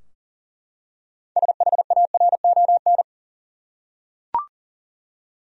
Morse code, heard as a keyed tone, about 40 wpm, 700 Hz. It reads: SHARON
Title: USA Common Names - 40wpm